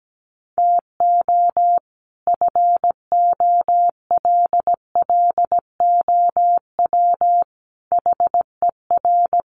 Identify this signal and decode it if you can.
Morse code, heard as a keyed tone, about 17 wpm, 700 Hz. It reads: TO FOLLOW HER
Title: Sentences from Top 1100 Words 17wpm